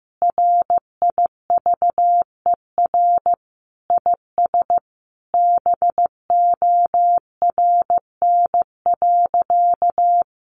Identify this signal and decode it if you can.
Morse code, heard as a keyed tone, about 15 wpm, 700 Hz. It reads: RIVER IS BORN.
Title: Sentences from Top 1200 Words 15wpm